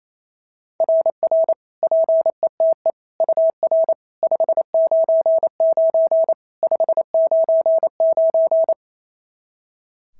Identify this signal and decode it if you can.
Morse code, heard as a keyed tone, about 28 wpm, 650 Hz. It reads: RR PETE UR 599 599
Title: SOTA QSO Practice 28wpm